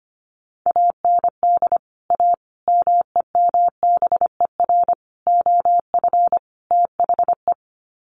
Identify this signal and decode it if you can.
Morse code, heard as a keyed tone, about 25 wpm, 700 Hz. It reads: AND A MEMBER OF THE
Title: Sets of 5 Words 25wpm